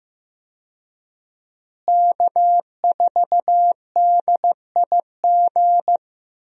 Morse code, heard as a keyed tone, about 15 wpm, 700 Hz. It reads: K4DIG
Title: USA Call Signs - 5 Characters - Repeated 2x - 15wpm